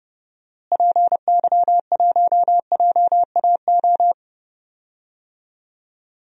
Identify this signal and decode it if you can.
Morse code, heard as a keyed tone, about 30 wpm, 700 Hz. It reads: PY1JAO
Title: DXCC-100 Call Signs - 6 Characters - Repeated 2x - 30wpm